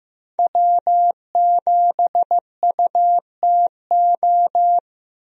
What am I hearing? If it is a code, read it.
Morse code, heard as a keyed tone, about 15 wpm, 700 Hz. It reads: W7UTO
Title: USA Call Signs - 5 Characters - Repeated 3x - 15wpm